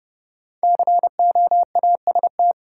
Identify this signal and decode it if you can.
Morse code, heard as a keyed tone, about 32 wpm, 700 Hz. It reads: COAST
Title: Sentences from Top 1000 Words - Review Part 4 - New Word Drill - 30wpm